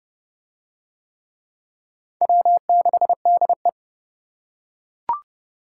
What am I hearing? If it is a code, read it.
Morse code, heard as a keyed tone, about 28 wpm, 700 Hz. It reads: W6DE